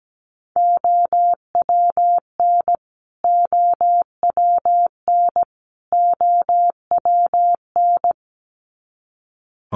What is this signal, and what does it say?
Morse code, heard as a keyed tone, about 17 wpm, 700 Hz. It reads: OWN OWN OWN
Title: Top 500 Words Repeated 17wpm